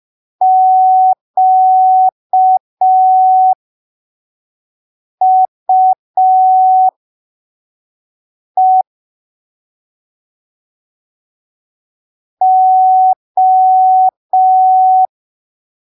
Morse code, heard as a keyed tone, about 5 wpm, spaced as for 3 wpm, 750 Hz. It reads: QUE O
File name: arquivos / texto / texto 17 05ppm